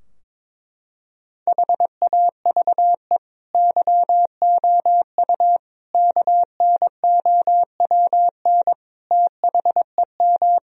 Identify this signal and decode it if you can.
Morse code, heard as a keyed tone, about 22 wpm, 700 Hz. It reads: HAVE YOU KNOWN THEM
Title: Sentences from Top 1000 Words - Review Part 3 - 22wpm